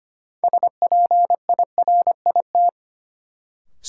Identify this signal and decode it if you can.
Morse code, heard as a keyed tone, about 25 wpm, 700 Hz. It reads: SPIRIT
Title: Sentences from Top 1400 Words - New Word Drill Alt - 25wpm